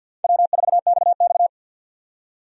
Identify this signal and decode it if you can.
Morse code, heard as a keyed tone, about 48 wpm, 700 Hz. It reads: W4XX
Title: USA Call Signs - 4 Characters - 50wpm